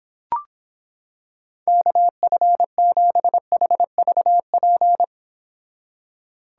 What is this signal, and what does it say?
Morse code, heard as a keyed tone, about 26 wpm, 700 Hz. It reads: KF7HVP